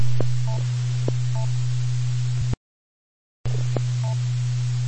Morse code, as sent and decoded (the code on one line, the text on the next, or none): none